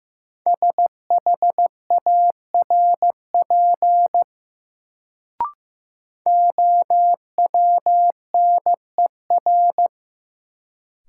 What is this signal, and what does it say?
Morse code, heard as a keyed tone, about 15 wpm, 700 Hz. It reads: SHARP OWNER
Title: Sentences from Top 1600 Words - New Word Drill - 15wpm